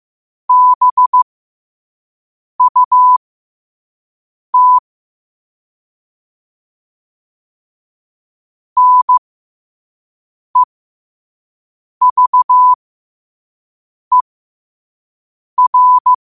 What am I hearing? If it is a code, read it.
Morse code, heard as a keyed tone, about 15 wpm, spaced as for 5 wpm, 1000 Hz. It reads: BUT NEVER
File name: qotdfiles / 5wpm